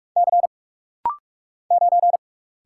Morse code, heard as a keyed tone, about 46 wpm, 700 Hz. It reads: C 9
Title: Single Letter-Number Alt - Rapid-Fire 45wpm